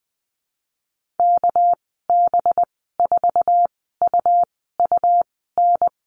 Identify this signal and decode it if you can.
Morse code, heard as a keyed tone, about 20 wpm, spaced as for 15 wpm, 700 Hz. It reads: KB4UUN
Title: 207 Lesson 33g - Letter j - Call Signs 20wpm